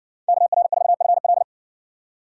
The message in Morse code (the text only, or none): KR4FL